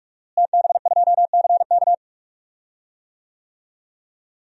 Morse code, heard as a keyed tone, about 46 wpm, 700 Hz. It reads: TB2CK
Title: DXCC-100 Call Signs - 5 Characters - Repeated 2x - 45wpm